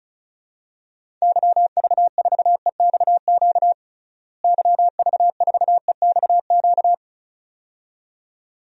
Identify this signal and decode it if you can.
Morse code, heard as a keyed tone, about 34 wpm, 700 Hz. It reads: YV4EXQ YV4EXQ
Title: DXCC-100 Call Signs - 6 Characters - Repeated 2x - 35wpm